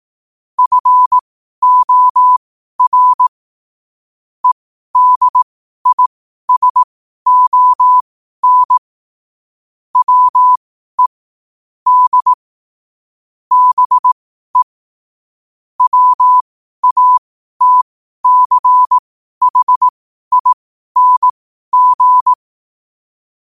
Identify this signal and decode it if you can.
Morse code, heard as a keyed tone, about 18 wpm, spaced as for 12 wpm, 1000 Hz. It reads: FOR EDISON WE D BE WATCHING
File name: qotdfiles / 12wpm